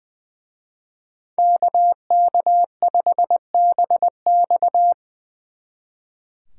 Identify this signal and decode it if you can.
Morse code, heard as a keyed tone, about 24 wpm, 700 Hz. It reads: KK5BX
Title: Speed Racing - USA Call Signs 20wpm